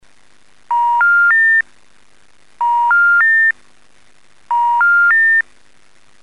Special information tone.
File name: voice / shenghuo